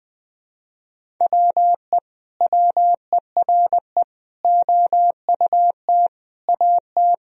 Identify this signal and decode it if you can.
Morse code, heard as a keyed tone, about 20 wpm, 700 Hz. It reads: WE WERE OUT AT
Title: Sentences from Top 1000 Words - Review Part 1 - 20wpm